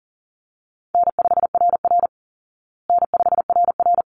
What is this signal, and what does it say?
Morse code, heard as a keyed tone, about 40 wpm, 700 Hz. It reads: N5RR N5RR
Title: Easy Call Signs Repeated 2x - 40wpm